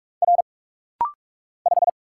Morse code, heard as a keyed tone, about 44 wpm, 700 Hz. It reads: R 5
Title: Single Letter-Number - Rapid-Fire 45wpm